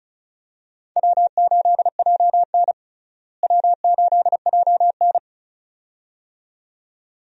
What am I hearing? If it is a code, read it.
Morse code, heard as a keyed tone, about 34 wpm, 700 Hz. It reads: W8JN W8JN
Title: USA Call Signs - 4 Characters - Repeated 2x - 35wpm